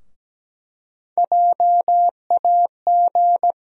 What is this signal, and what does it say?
Morse code, heard as a keyed tone, about 17 wpm, 700 Hz. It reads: JAG
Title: USA Call Sign Postfixes - 3 Letters 17wpm